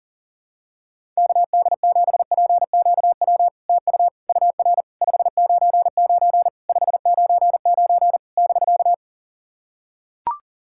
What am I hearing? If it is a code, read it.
Morse code, heard as a keyed tone, about 40 wpm, 700 Hz. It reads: KD7PQW TU UR 599 599 <BK>
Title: SOTA QSO Practice 40wpm